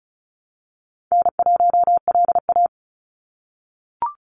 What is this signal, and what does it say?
Morse code, heard as a keyed tone, about 34 wpm, 700 Hz. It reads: N1LA
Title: USA Call Signs 35wpm